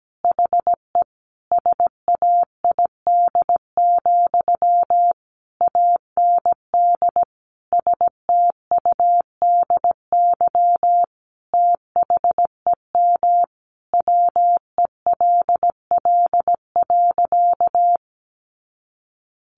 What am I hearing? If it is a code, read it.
Morse code, heard as a keyed tone, about 17 wpm, 700 Hz. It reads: HE SAID, AND STUDY THEM WELL.
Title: Sentences from Top 300 Words 17wpm